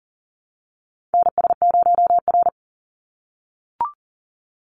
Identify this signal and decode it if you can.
Morse code, heard as a keyed tone, about 40 wpm, 700 Hz. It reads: NS0R
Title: Call Signs - 40wpm